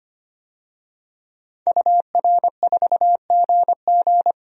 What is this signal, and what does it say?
Morse code, heard as a keyed tone, about 25 wpm, 690 Hz. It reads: UR4GG
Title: DXCC-100 Call Signs - 5 Characters - 25wpm